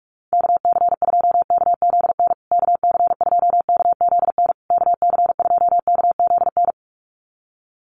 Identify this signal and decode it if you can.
Morse code, heard as a keyed tone, about 44 wpm, 690 Hz. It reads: KC2KZN KC2KZN KC2KZN
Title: USA Call Signs Repeated 3x 45wpm